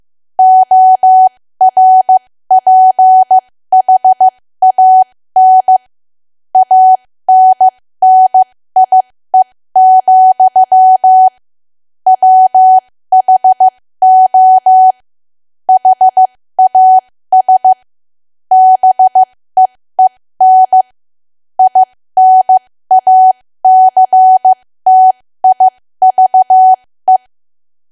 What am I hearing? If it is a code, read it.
Morse code, heard as a keyed tone, about 15 wpm, spaced as for 13 wpm, 750 Hz. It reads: ORPHAN ANNIE, WHO HAS BEEN INACTIVE
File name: cwa / W1AW-Level2 / 13WPM